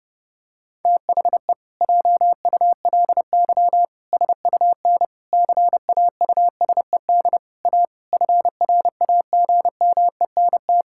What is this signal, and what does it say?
Morse code, heard as a keyed tone, about 30 wpm, 700 Hz. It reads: THE JULY SUN CAUSED A FRAGMENT
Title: Pangram Challenge v2 - 30wpm